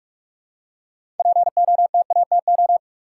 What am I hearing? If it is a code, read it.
Morse code, heard as a keyed tone, about 46 wpm, 700 Hz. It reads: POTATO
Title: Six Letter Words - 45wpm